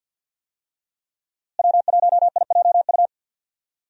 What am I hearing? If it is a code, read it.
Morse code, heard as a keyed tone, about 52 wpm, 700 Hz. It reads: W1IJU
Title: USA Call Signs - 5 Characters - Repeated 3x - 50wpm